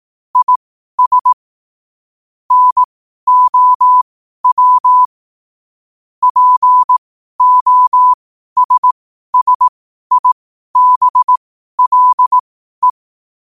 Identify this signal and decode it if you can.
Morse code, heard as a keyed tone, about 18 wpm, spaced as for 12 wpm, 1000 Hz. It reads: IS NOW POSSIBLE
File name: qotdfiles / 12wpm